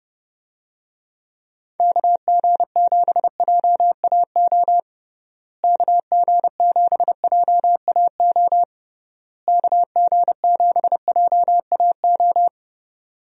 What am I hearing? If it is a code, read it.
Morse code, heard as a keyed tone, about 30 wpm, 700 Hz. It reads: KG7JAO KG7JAO KG7JAO
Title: USA Call Signs - 6 Characters - Repeated 3x - 30wpm